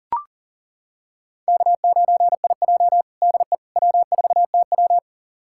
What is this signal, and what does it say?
Morse code, heard as a keyed tone, about 40 wpm, 700 Hz. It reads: K9IJ DE W4TW
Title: USA Callsign DE Callsign - 4 Characters - 40wpm